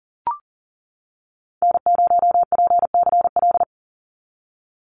Morse code, heard as a keyed tone, about 38 wpm, 700 Hz. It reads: N0PCL
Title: NA SOTA Activator Callsigns 40wpm